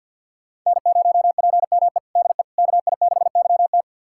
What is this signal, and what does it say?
Morse code, heard as a keyed tone, about 52 wpm, 700 Hz. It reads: N0PME DE KI6YT
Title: USA Callsign DE Callsign - 5 Characters - 50wpm